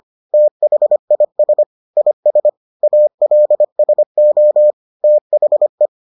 Morse code, heard as a keyed tone, about 25 wpm, 600 Hz. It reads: THIS IS ALSO THE